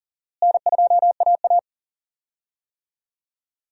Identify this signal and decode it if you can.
Morse code, heard as a keyed tone, about 40 wpm, 700 Hz. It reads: N2AA
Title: USA Call Signs - 4 Characters - Repeated 3x - 40wpm